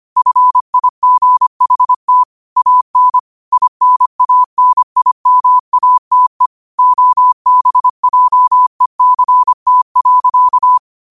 Morse code, heard as a keyed tone, about 25 wpm, 1000 Hz. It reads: FIGHT AN INANIMATE OBJECT.
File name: qotdfiles / 25wpm